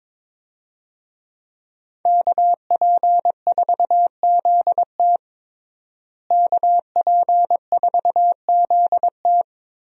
Morse code, heard as a keyed tone, about 22 wpm, 700 Hz. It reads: KP4ZT KP4ZT
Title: USA Call Signs - 5 Characters - Repeated 2x - 22wpm